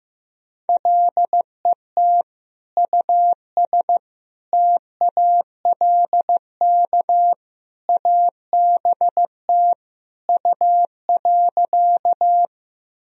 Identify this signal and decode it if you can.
Morse code, heard as a keyed tone, about 15 wpm, 700 Hz. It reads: LET US TALK ABT U.
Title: Sentences from Top 400 Words and Abbreviations 15wpm